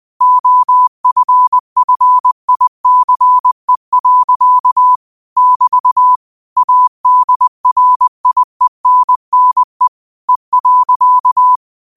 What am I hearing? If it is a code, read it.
Morse code, heard as a keyed tone, about 20 wpm, 1000 Hz. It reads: OFFICE. <BT> ADRIENNE E.
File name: qotdfiles / 20wpm